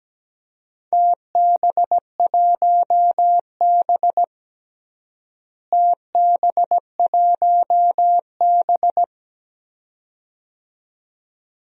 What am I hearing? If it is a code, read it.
Morse code, heard as a keyed tone, about 17 wpm, 700 Hz. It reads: TB1B TB1B
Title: DXCC-100 Call Signs - 4 Characters - Repeated 2x - 17wpm